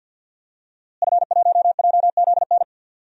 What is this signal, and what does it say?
Morse code, heard as a keyed tone, about 51 wpm, 700 Hz. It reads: F1JZN
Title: DXCC-100 Call Signs - 5 Characters - Repeated 2x - 50wpm